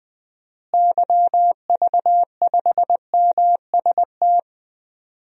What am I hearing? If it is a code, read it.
Morse code, heard as a keyed tone, about 20 wpm, 700 Hz. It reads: YV5MST
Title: DXCC-100 Call Signs - 6 Characters - Repeated 2x - 20wpm